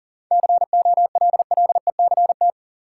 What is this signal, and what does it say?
Morse code, heard as a keyed tone, about 37 wpm, 700 Hz. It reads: COLLECT
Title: Sentences from Top 1600 Words - New Word Drill - 40wpm